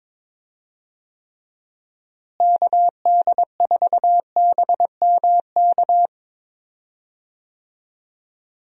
Morse code, heard as a keyed tone, about 21 wpm, 700 Hz. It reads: KD4BMK
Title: USA Call Signs - 6 Characters - Repeated 3x - 22wpm